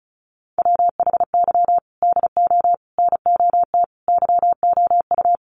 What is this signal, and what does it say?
Morse code, heard as a keyed tone, about 34 wpm, 700 Hz. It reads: WHY DO NOT YOU
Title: Sets of 5 Words Encore2 35wpm